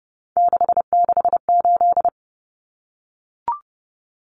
Morse code, heard as a keyed tone, about 30 wpm, 700 Hz. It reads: 668
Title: Numbers - 3-Digit 30wpm